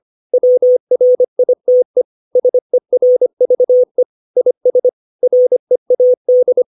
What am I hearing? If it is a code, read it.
Morse code, heard as a keyed tone, about 25 wpm, 500 Hz. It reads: WRITE SERVE IS READ
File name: 632 words / 632 Words 25WPM